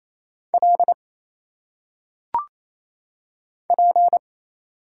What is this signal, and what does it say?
Morse code, heard as a keyed tone, about 27 wpm, 700 Hz. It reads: L P
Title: Single Letters - Phonetic Pronunciation 28wpm